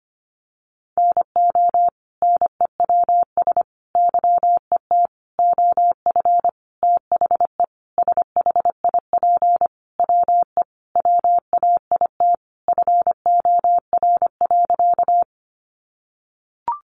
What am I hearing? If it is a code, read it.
Morse code, heard as a keyed tone, about 25 wpm, 700 Hz. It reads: NO NEWS YET OF THE SHIP WE WAIT FOR.
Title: Sentences from Top 800 Words 25wpm